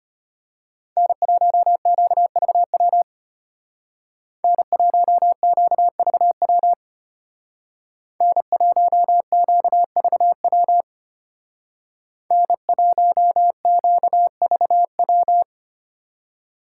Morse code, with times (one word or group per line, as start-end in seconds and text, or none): 1.0-3.0 N1QVW
4.4-6.7 N1QVW
8.2-10.8 N1QVW
12.3-15.4 N1QVW